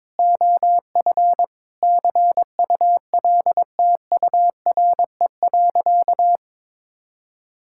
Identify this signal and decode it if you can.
Morse code, heard as a keyed tone, about 22 wpm, 700 Hz. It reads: OF CULTURE.
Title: Sentences from Top 1000 Words - Review Part 6 - 22wpm